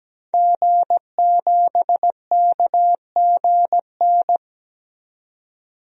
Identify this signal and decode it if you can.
Morse code, heard as a keyed tone, about 17 wpm, 700 Hz. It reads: G7KGN
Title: DXCC-100 Call Signs - 5 Characters - 17wpm